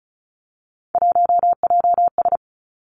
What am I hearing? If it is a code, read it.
Morse code, heard as a keyed tone, about 36 wpm, 700 Hz. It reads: 1JS
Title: Three Challenging Letter-Number Combinations 35wpm